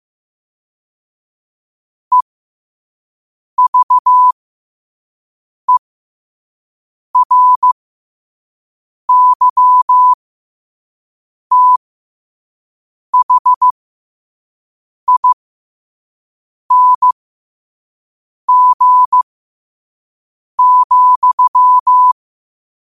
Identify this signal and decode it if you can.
Morse code, heard as a keyed tone, about 15 wpm, spaced as for 5 wpm, 1000 Hz. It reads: EVERYTHING,
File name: qotdfiles / 5wpm